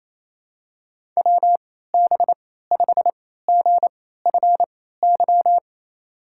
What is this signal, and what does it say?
Morse code, heard as a keyed tone, about 28 wpm, spaced as for 16 wpm, 700 Hz. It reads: WB5GFY
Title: USA Call Signs - 3x Character Spacing - 28wpm